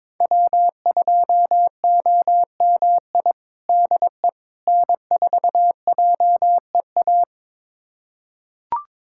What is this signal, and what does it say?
Morse code, heard as a keyed tone, about 22 wpm, 700 Hz. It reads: W2OMI DE N4JEA
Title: USA Callsign DE Callsign - 5 Characters - 22wpm